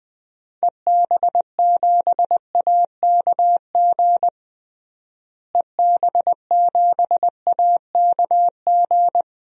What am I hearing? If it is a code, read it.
Morse code, heard as a keyed tone, about 20 wpm, 700 Hz. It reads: EB7AKG EB7AKG
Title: DXCC-100 Call Signs - 6 Characters - Repeated 3x - 20wpm